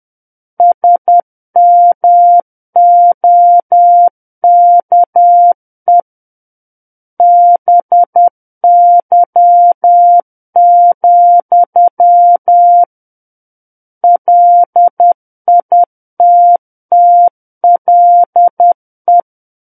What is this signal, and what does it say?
Morse code, heard as a keyed tone, about 10 wpm, 700 Hz. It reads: SMOKE BY, LITTLE
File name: Book One / 01-Chapter 1 - 10 WPM